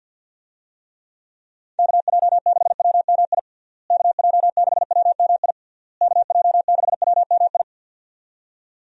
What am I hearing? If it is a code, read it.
Morse code, heard as a keyed tone, about 52 wpm, 700 Hz. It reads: KJ6WMI KJ6WMI KJ6WMI
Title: USA Call Signs - 6 Characters - Repeated 3x - 50wpm